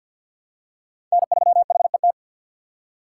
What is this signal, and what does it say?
Morse code, heard as a keyed tone, about 47 wpm, 700 Hz. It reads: N3HET
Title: USA Call Signs - 5 Characters - 3x Character Spacing - 50wpm